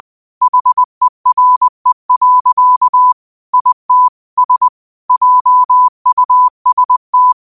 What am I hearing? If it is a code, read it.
Morse code, heard as a keyed tone, about 20 wpm, 1000 Hz. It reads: HERE. IT S JUST
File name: qotdfiles / 20wpm